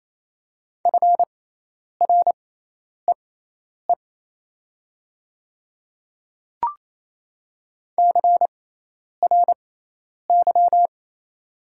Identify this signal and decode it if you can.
Morse code, heard as a keyed tone, about 28 wpm, spaced as for 10 wpm, 700 Hz. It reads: FREE CRY
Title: 6x Character Spacing - Top 500 Words - 28wpm